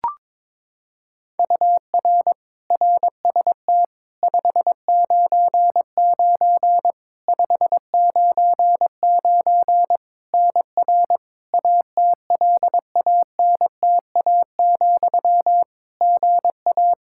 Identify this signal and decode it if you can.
Morse code, heard as a keyed tone, about 22 wpm, 700 Hz. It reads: UR RST 599 599 NR ATLANTA, GA
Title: Practice QSO - Exchange 1 - Beginning - 22wpm